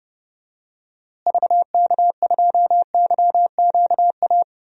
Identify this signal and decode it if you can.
Morse code, heard as a keyed tone, about 30 wpm, 700 Hz. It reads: VK2YQA
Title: DXCC-100 Call Signs - 6 Characters - Repeated 2x - 30wpm